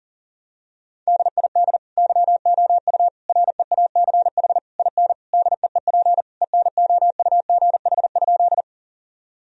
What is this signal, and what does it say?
Morse code, heard as a keyed tone, about 37 wpm, 700 Hz. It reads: DID YOU REACH IN DEEP ENOUGH?
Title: Sentences from Top 1000 Words - Review Part 3 - 40wpm